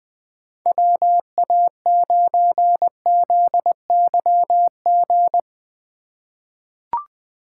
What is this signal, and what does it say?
Morse code, heard as a keyed tone, about 20 wpm, 700 Hz. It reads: WA9ZYG
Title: USA Call Signs - 6 Characters - 3x Character Spacing - 20wpm